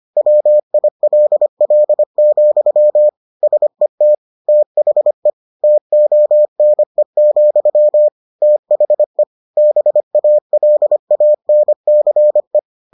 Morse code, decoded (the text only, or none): WILL, SET THE TONE, THE BALANCE